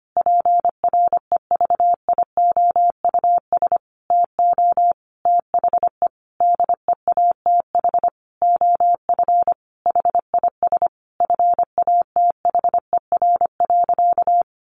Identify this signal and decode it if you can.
Morse code, heard as a keyed tone, about 25 wpm, 700 Hz. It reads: PREVIOUS TO THE DEATH OF HIS FATHER.
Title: Sentences from Top 1500 Words 25wpm